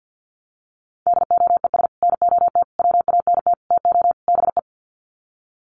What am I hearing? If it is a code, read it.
Morse code, heard as a keyed tone, about 50 wpm, 700 Hz. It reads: DOES NOT WANT TO BE
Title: Sets of 5 Words Encore 50wpm